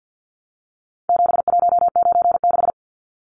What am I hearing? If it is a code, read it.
Morse code, heard as a keyed tone, about 48 wpm, 700 Hz. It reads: 7196
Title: Numbers - 4-Digit 50wpm